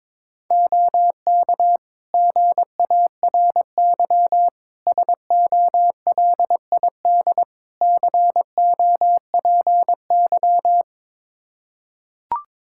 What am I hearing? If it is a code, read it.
Morse code, heard as a keyed tone, about 22 wpm, 700 Hz. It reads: OK GARY SOLID COPY
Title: Practice QSO - Exchange 3 - More Info - 22wpm